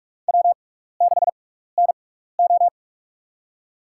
Morse code, heard as a keyed tone, about 46 wpm, spaced as for 16 wpm, 700 Hz. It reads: W6NO
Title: USA Call Signs - 6x Character Spacing - 45wpm